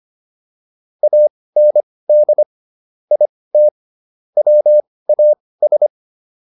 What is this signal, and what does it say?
Morse code, heard as a keyed tone, about 25 wpm, spaced as for 18 wpm, 600 Hz. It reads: AND IT WAS